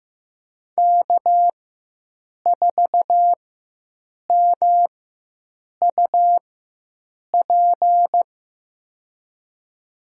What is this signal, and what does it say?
Morse code, heard as a keyed tone, about 15 wpm, spaced as for 7 wpm, 700 Hz. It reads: K4MUP
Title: USA Call Signs - 5 Characters - 4x Character Spacing - 15wpm